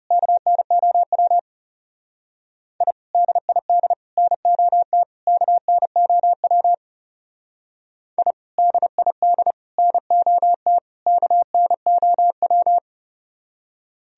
Morse code, heard as a keyed tone, about 34 wpm, 700 Hz. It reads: KNOW I DID NOT KNOW I DID NOT KNOW
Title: Speed-Racing - Sets of 4 Words 30wpm